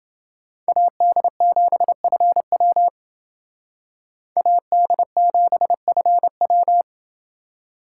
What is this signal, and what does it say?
Morse code, heard as a keyed tone, about 29 wpm, 700 Hz. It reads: AD7FW AD7FW